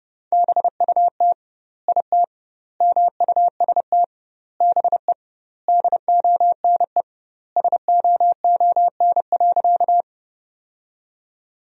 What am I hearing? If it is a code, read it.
Morse code, heard as a keyed tone, about 31 wpm, 700 Hz. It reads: BUT IT MUST BE DONE SOON.
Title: 2x Word-Spacing - Sentences from Top 500 Words - 30wpm